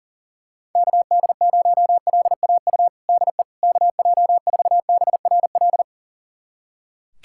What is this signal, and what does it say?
Morse code, heard as a keyed tone, about 40 wpm, 700 Hz. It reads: KD0LAU DE KJ4BRL
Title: USA Callsign DE Callsign 40wpm